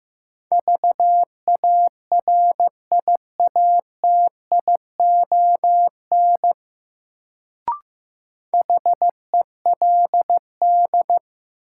Morse code, heard as a keyed tone, about 15 wpm, 700 Hz. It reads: VARIATION HELD
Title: Sentences from Top 1600 Words - New Word Drill Alt - 15wpm